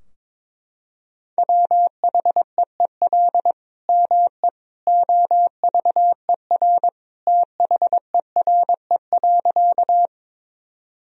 Morse code, heard as a keyed tone, about 22 wpm, 700 Hz. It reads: WHEEL ME OVER THERE.
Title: Sentences from Top 1000 Words - Review Part 3 - 22wpm